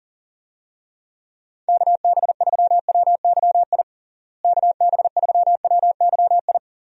Morse code, heard as a keyed tone, about 40 wpm, 700 Hz. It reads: KB3WYI KB3WYI
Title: USA Call Signs - 6 Characters - Repeated 3x - 40wpm